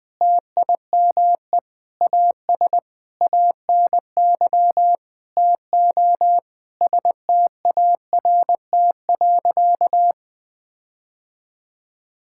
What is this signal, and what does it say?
Morse code, heard as a keyed tone, about 21 wpm, 700 Hz. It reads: TIME AS ANY TO START.
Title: Sentences with 8-Word Phrases - 20wpm